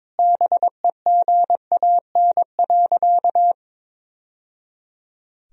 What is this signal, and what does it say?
Morse code, heard as a keyed tone, about 22 wpm, 700 Hz. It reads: BEGAN.
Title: Sentences from Top 1000 Words - Review Part 1 - 22wpm